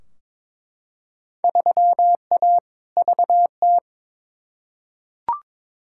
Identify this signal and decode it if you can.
Morse code, heard as a keyed tone, about 22 wpm, 700 Hz. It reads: 3A VT
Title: Field Day Exchanges Encore - 22wpm